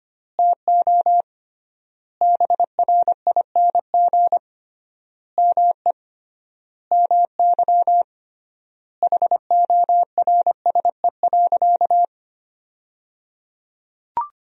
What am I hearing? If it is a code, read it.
Morse code, heard as a keyed tone, about 25 wpm, 700 Hz. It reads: TO BRING ME MY HORSE.
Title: 3x Word-Spacing - Sentences from Top 500 Words - 25wpm